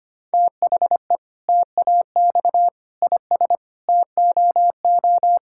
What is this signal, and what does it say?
Morse code, heard as a keyed tone, about 25 wpm, 700 Hz. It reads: THE TAX IS TOO
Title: Sentences from Top 1000 Words - Review Part 5 - 25wpm